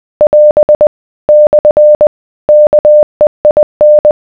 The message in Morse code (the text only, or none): <AS> / KEIN